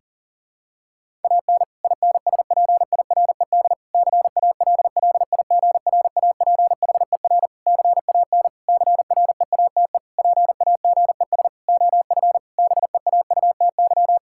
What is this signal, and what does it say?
Morse code, heard as a keyed tone, about 40 wpm, 700 Hz. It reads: AN INSPIRED CALLIGRAPHER CAN CREATE PAGES OF BEAUTY
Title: Pangram Challenge v2 - 40wpm